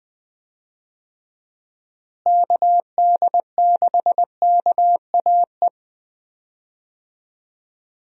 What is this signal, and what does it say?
Morse code, heard as a keyed tone, about 21 wpm, 700 Hz. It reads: KD6KAE
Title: USA Call Signs - 6 Characters - Repeated 2x - 20wpm